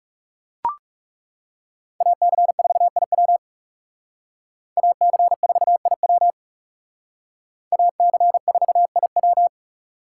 Morse code, heard as a keyed tone, about 39 wpm, 700 Hz. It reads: AC4IW AC4IW AC4IW